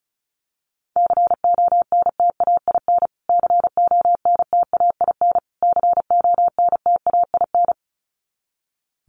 Morse code, heard as a keyed tone, about 34 wpm, 700 Hz. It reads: CONTAIN CONTAIN CONTAIN
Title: Top 500 Words Repeated 35wpm